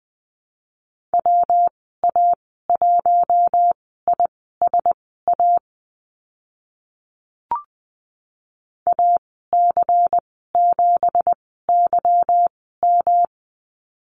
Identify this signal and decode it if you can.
Morse code, heard as a keyed tone, about 20 wpm, spaced as for 14 wpm, 700 Hz. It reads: WA1ISA AC7YM
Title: 179 Lesson 29g - Slash - Call Signs 20wpm